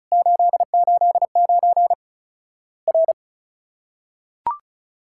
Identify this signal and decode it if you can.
Morse code, heard as a keyed tone, about 34 wpm, 700 Hz. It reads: 889 R
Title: Worked All Europe DX Contest - QTC Exchange - 35wpm